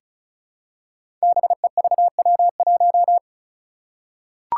Morse code, heard as a keyed tone, about 36 wpm, 700 Hz. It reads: BEVW1
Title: Five Challenging Letter-Number Combinations 35wpm